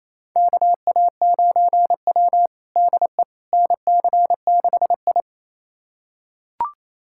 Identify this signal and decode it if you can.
Morse code, heard as a keyed tone, about 28 wpm, 700 Hz. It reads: KA9W DE NC6I
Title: USA Callsign DE Callsign - 4 Characters - 28wpm